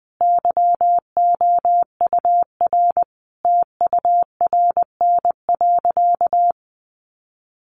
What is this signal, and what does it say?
Morse code, heard as a keyed tone, about 19 wpm, 700 Hz. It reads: YOUR TURN.
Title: Sentences from Top 600 Words 20wpm